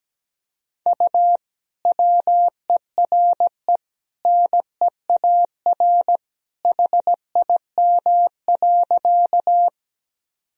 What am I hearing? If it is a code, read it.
Morse code, heard as a keyed tone, about 17 wpm, 700 Hz. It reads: U WERE NEAR HIM.
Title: Sentences from Top 200 Words and Abbreviations 17wpm